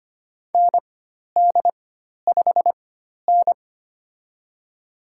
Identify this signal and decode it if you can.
Morse code, heard as a keyed tone, about 25 wpm, spaced as for 12 wpm, 690 Hz. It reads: ND5N
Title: USA Call Signs - 4 Characters - 4x Character Spacing - 25wpm